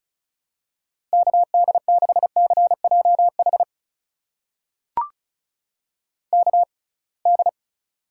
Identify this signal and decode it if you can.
Morse code, heard as a keyed tone, about 34 wpm, spaced as for 26 wpm, 700 Hz. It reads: KD6CJH KD
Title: USA Call Signs - 6 Characters - 6x Character Spacing - 35wpm